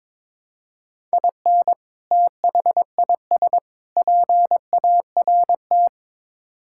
Morse code, heard as a keyed tone, about 21 wpm, 700 Hz. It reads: IN THIS PART
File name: practice / 22wpm / Build A Phrase - 5-Word Phrases - 22wpm